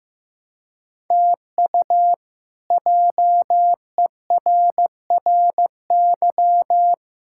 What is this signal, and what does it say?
Morse code, heard as a keyed tone, about 16 wpm, 700 Hz. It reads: TU JERRY
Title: K1USN SST QSO Practice - 15wpm